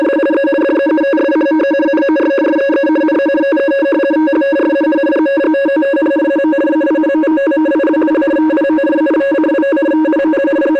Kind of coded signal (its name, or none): two-tone teleprinter signal